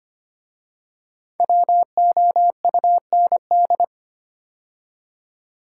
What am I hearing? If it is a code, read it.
Morse code, heard as a keyed tone, about 25 wpm, 700 Hz. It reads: WOUND
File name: 5 Letter Words - First 3 Letters Given - 25wpm